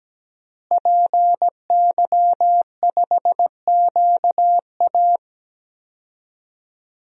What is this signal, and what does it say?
Morse code, heard as a keyed tone, about 17 wpm, 700 Hz. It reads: PY5QA
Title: DXCC-100 Call Signs - 5 Characters - Repeated 3x - 17wpm